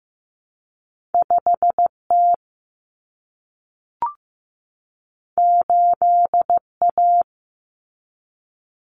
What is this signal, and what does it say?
Morse code, heard as a keyed tone, about 15 wpm, 700 Hz. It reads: 5T 8A
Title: Common Cut Numbers - 2-Digit 15wpm